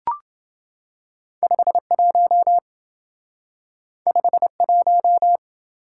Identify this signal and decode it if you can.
Morse code, heard as a keyed tone, about 27 wpm, 700 Hz. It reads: 51 51